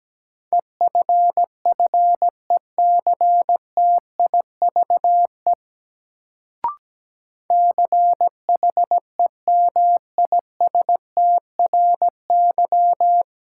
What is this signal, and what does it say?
Morse code, heard as a keyed tone, about 17 wpm, 700 Hz. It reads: EFFECTIVE CHEMISTRY
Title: Sentences from Top 1000 Words - Review Part 6 - New Word Drill - 17wpm